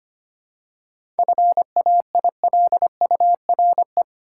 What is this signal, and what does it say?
Morse code, heard as a keyed tone, about 25 wpm, 700 Hz. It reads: FAILURE
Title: Seven Letter Words - 25wpm